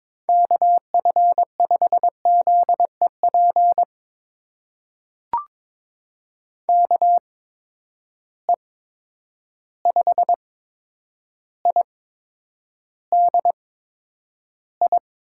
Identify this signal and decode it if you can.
Morse code, heard as a keyed tone, about 22 wpm, spaced as for 9 wpm, 700 Hz. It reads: KF5ZEP KE5IDI
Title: USA Call Signs - 6 Characters - 8x Character Spacing - 22wpm